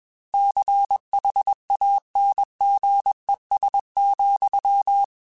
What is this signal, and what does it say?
Morse code, heard as a keyed tone, about 21 wpm, 800 Hz. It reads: CHANGES,
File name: Technical Subjects / Technical Subjects-mp3s / Reactance @20